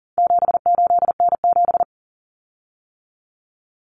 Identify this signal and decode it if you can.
Morse code, heard as a keyed tone, about 40 wpm, 700 Hz. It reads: 78N7
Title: Common Cut Numbers - 4-Digit 40wpm